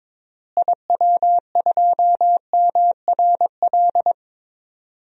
Morse code, heard as a keyed tone, about 22 wpm, 700 Hz. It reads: IW2MRL
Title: DXCC-100 Call Signs - 6 Characters - Repeated 3x - 22wpm